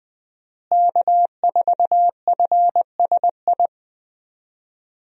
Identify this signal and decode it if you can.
Morse code, heard as a keyed tone, about 20 wpm, 700 Hz. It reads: K4FSI
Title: USA Call Signs - 5 Characters - Repeated 2x - 20wpm